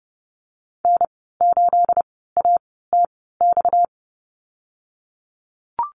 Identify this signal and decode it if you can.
Morse code, heard as a keyed tone, about 30 wpm, spaced as for 17 wpm, 700 Hz. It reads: N8ATX
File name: Lesson 37g - Letter x - Call Signs 30wpm